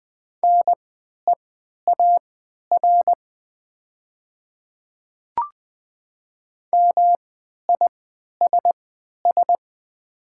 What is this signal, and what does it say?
Morse code, heard as a keyed tone, about 20 wpm, spaced as for 11 wpm, 700 Hz. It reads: NEAR MISS
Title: 3x Character Spacing - Top 500 Words - 20wpm